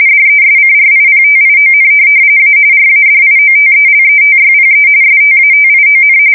RTTY, about 45.45 baud. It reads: DESCRIPTION A RADIOTELETYPE STATION
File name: fsk170 / teletype-RTTY